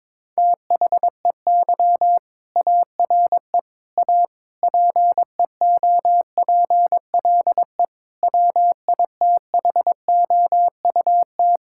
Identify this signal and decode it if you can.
Morse code, heard as a keyed tone, about 21 wpm, 700 Hz. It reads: THEY ARE A PEOPLE WITHOUT
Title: Sentences from Top 1000 Words - Review Part 6 - 22wpm